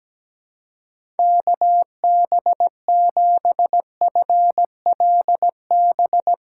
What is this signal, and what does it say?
Morse code, heard as a keyed tone, about 17 wpm, 700 Hz. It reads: KB7FLB
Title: USA Call Signs - 6 Characters - 3x Character Spacing - 17wpm